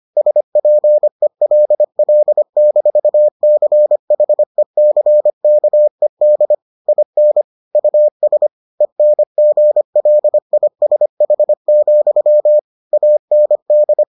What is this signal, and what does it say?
Morse code, heard as a keyed tone, about 25 wpm, 600 Hz. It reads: SPELL-CHECKED IN US ENGLISH, AND